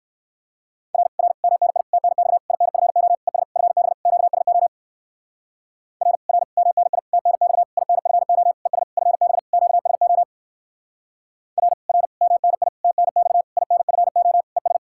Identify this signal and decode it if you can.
Morse code, heard as a keyed tone, about 59 wpm, 700 Hz. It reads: R R MNI TNX INFO ES FB QSO R R MNI TNX INFO ES FB QSO R R MNI TNX INFO ES